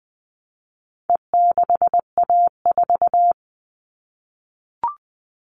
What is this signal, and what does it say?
Morse code, heard as a keyed tone, about 20 wpm, 700 Hz. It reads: E6A4
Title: All Cut Numbers - 4-Digit 20wpm